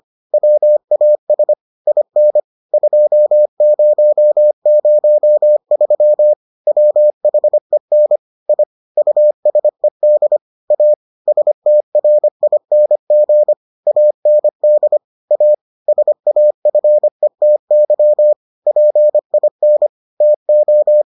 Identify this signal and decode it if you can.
Morse code, heard as a keyed tone, about 25 wpm, 600 Hz. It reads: WAS IN 2003 WHEN I USED A STRING AND A SAFETY PIN TO